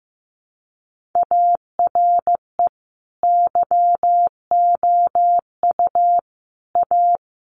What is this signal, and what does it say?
Morse code, heard as a keyed tone, about 15 wpm, 700 Hz. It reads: ARE YOU A
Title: Sentences from Top 1100 Words 15wpm